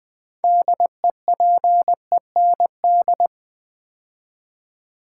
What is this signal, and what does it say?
Morse code, heard as a keyed tone, about 20 wpm, 700 Hz. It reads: DEPEND
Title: 6 Letter Words - First 3 Letters Given - 20wpm